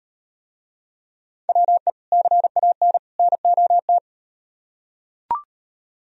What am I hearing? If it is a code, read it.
Morse code, heard as a keyed tone, about 38 wpm, 700 Hz. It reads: WE CAN NOT